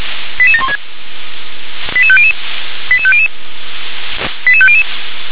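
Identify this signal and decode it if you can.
Five-tone selective call (ZVEI), heard as quick runs of five tones, tones of about 70 ms. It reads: 88716 88500 88500 88500
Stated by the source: ZVEI five-tone selective call received via SDR.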